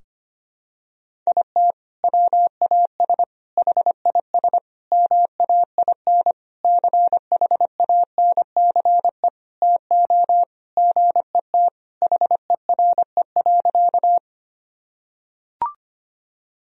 Morse code, heard as a keyed tone, about 25 wpm, 700 Hz. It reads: IT WAS HIS MAIN CHANCE TO GET HERE.
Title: Sentences from Top 1000 Words - Review Part 2 - 25wpm